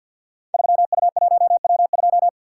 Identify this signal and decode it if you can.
Morse code, heard as a keyed tone, about 46 wpm, 700 Hz. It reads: 3U1W2
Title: Code Groups - 5 Letter:Numbers - 50wpm